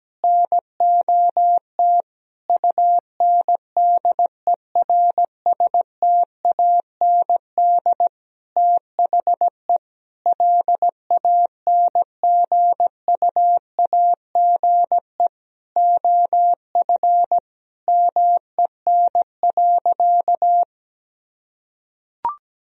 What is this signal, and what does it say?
Morse code, heard as a keyed tone, about 17 wpm, 700 Hz. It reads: NOT UNDERSTAND THE LANGUAGE OF MEN.
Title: Sentences from Top 1000 Words - Review Part 3 - 17wpm